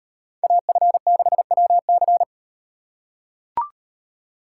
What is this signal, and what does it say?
Morse code, heard as a keyed tone, about 37 wpm, 700 Hz. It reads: AF6WC